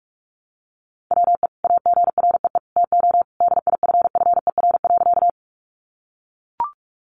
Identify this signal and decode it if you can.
Morse code, heard as a keyed tone, about 44 wpm, 700 Hz. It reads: WE AGREE TO DIFFER.
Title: Sentences from Top 1100 Words 45wpm